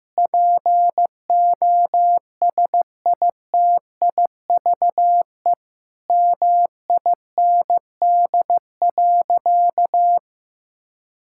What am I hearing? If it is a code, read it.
Morse code, heard as a keyed tone, about 16 wpm, 700 Hz. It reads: POSITIVE MIND.
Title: Sentences from Top 1000 Words - Review Part 5 - 15wpm